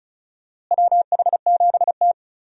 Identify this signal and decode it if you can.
Morse code, heard as a keyed tone, about 36 wpm, 700 Hz. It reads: WH7T
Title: USA Call Signs - 4 Characters - Repeated 2x - 35wpm